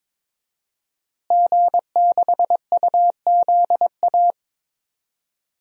Morse code, heard as a keyed tone, about 22 wpm, 700 Hz. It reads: G6UZA
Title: DXCC-100 Call Signs - 5 Characters - Repeated 2x - 22wpm